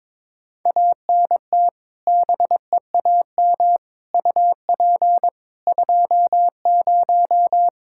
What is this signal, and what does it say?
Morse code, heard as a keyed tone, about 22 wpm, 700 Hz. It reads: ANT BEAM UP 20
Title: Practice QSO - Exchange 2 - Info - 22wpm